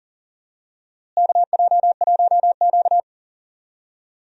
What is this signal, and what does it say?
Morse code, heard as a keyed tone, about 39 wpm, 700 Hz. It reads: KJ1Q
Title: USA Call Signs - 4 Characters - Repeated 2x - 40wpm